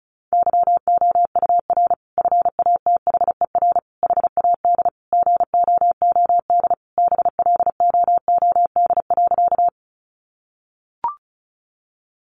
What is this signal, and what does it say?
Morse code, heard as a keyed tone, about 34 wpm, 700 Hz. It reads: YOUR FATHER HAD GOOD BLOOD.
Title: Sentences from Top 1100 Words 35wpm